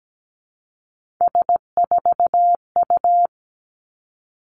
Morse code, heard as a keyed tone, about 17 wpm, 700 Hz. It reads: S4U
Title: Three Challenging Letter-Number Combinations 17wpm